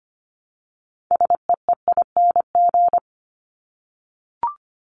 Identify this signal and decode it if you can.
Morse code, heard as a keyed tone, about 25 wpm, 700 Hz. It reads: SEEING
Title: ING Suffix - Top 100 Words with Base Words 25wpm